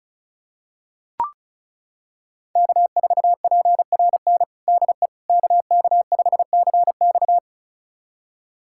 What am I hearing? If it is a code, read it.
Morse code, heard as a keyed tone, about 34 wpm, 700 Hz. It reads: K4PRN DE KK5CX
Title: USA Callsign DE Callsign - 5 Characters - 35wpm